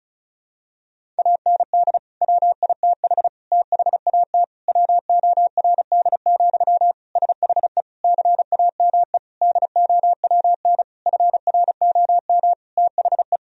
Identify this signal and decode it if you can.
Morse code, heard as a keyed tone, about 36 wpm, 700 Hz. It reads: AND WITH THAT WORD, SHE CAME DOWN FROM THE
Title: Sentences from Top 1000 Words - Review Part 1 - 35wpm